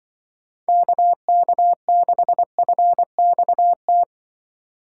Morse code, heard as a keyed tone, about 24 wpm, 700 Hz. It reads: KK6FXT
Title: Speed Racing - USA Call Signs 20wpm